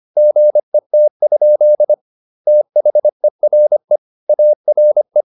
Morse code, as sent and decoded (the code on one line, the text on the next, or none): --. - ..--.. / - .... .-. / .- .-. .
GET? THERE ARE